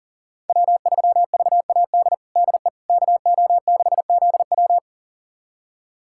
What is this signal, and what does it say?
Morse code, heard as a keyed tone, about 40 wpm, 700 Hz. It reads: W3VAD DE KO6ZW
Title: USA Callsign DE Callsign - 5 Characters - 40wpm